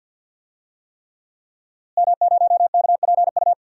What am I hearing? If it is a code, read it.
Morse code, heard as a keyed tone, about 46 wpm, 700 Hz. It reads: M0KPU
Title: DXCC-100 Call Signs - 5 Characters - Repeated 3x - 50wpm